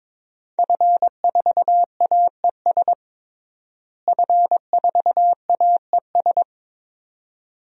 Morse code, heard as a keyed tone, about 22 wpm, 700 Hz. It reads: F4AES F4AES
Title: DXCC-100 Call Signs - 5 Characters - Repeated 2x - 22wpm